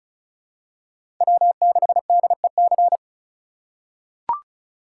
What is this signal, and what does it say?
Morse code, heard as a keyed tone, about 36 wpm, 700 Hz. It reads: W6DEC
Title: USA Call Signs - 5 Characters - 3x Character Spacing - 35wpm